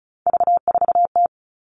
Morse code, heard as a keyed tone, about 36 wpm, 700 Hz. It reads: V4T
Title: Three Challenging Letter-Number Combinations 35wpm